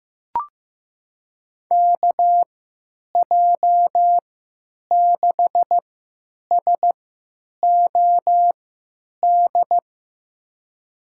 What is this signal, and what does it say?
Morse code, heard as a keyed tone, about 15 wpm, spaced as for 9 wpm, 700 Hz. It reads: KJ6SOD
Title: USA Call Signs - 6 Characters - 3x Character Spacing - 15wpm